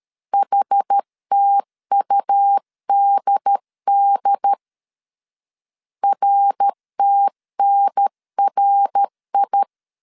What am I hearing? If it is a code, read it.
Morse code, heard as a keyed tone, about 13 wpm, 800 Hz. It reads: HTUDD RTNRI
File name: CW / DET / 1 3 1500C S A 12wpm 2905